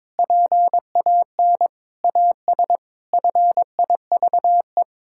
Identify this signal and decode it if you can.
Morse code, heard as a keyed tone, about 22 wpm, 700 Hz. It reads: PAN AS FIVE
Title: Pangram Challenge v2 - 22wpm